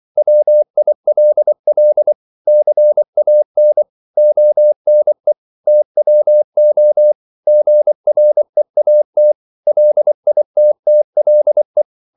Morse code, heard as a keyed tone, about 24 wpm, 600 Hz. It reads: WILL CAN ONE TWO GREAT LITTLE
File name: CW / Practice / 100MostCommonEnglishWords24